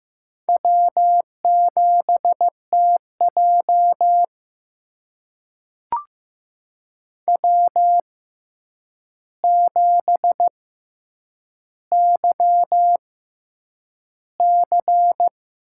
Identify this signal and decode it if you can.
Morse code, heard as a keyed tone, about 15 wpm, spaced as for 8 wpm, 700 Hz. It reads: W7TJ W7YC
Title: USA Call Signs - 4 Characters - 6x Character Spacing - 15wpm